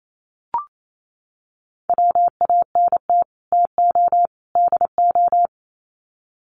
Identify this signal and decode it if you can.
Morse code, heard as a keyed tone, about 28 wpm, 700 Hz. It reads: WANT TO DO
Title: Sets of 3 Words 28wpm